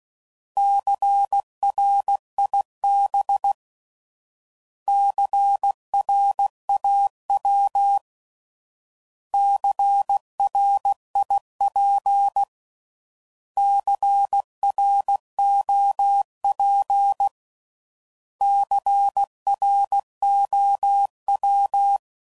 Morse code, heard as a keyed tone, about 16 wpm, 800 Hz. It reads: CRIB CRAW CRIP CROP CROW
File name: Prefixes / By First Letter / Cs